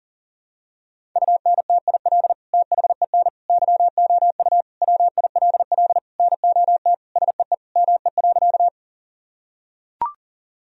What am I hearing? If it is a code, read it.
Morse code, heard as a keyed tone, about 40 wpm, 700 Hz. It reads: UNTIL THEN YOU WILL NOT SEE ME.
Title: Sentences from Top 1000 Words - Review Part 1 - 40wpm